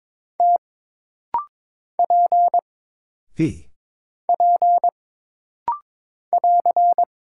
Morse code, heard as a keyed tone, about 22 wpm, 700 Hz. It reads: T P P <AR>
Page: This rapid-fire timing is geared to push your ICR (Instant Character Recognition) skills.